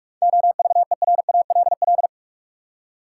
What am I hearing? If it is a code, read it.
Morse code, heard as a keyed tone, about 46 wpm, 700 Hz. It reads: OVERALL
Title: Seven Letter Words - 45wpm